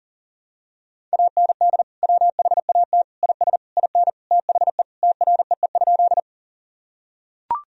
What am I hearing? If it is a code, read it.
Morse code, heard as a keyed tone, about 37 wpm, 700 Hz. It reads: AND WHAT IS IN THE TREE?
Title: Sentences from Top 1000 Words - Review Part 1 - 40wpm